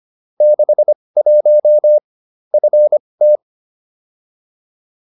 Morse code, heard as a keyed tone, about 25 wpm, spaced as for 20 wpm, 600 Hz. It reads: 61 FT
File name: QSO310a 20